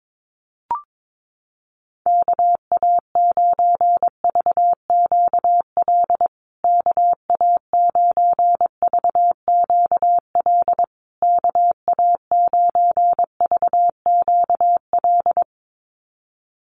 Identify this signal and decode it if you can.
Morse code, heard as a keyed tone, about 22 wpm, 700 Hz. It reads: KA9VQL KA9VQL KA9VQL
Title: USA Call Signs Repeated 3x 22wpm